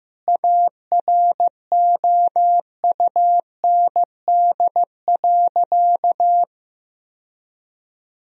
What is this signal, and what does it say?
Morse code, heard as a keyed tone, about 15 wpm, 700 Hz. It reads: AROUND.
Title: Sentences from Top 1000 Words - Review Part 3 - 15wpm